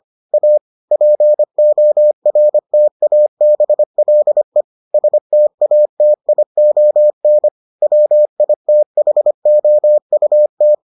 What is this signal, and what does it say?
Morse code, heard as a keyed tone, about 25 wpm, 600 Hz. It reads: A PORTABLE STATION WITHOUT